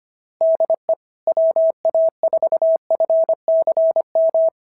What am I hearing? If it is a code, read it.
Morse code, heard as a keyed tone, about 24 wpm, 650 Hz. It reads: DE WA4FCM
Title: SOTA QSO Practice 25wpm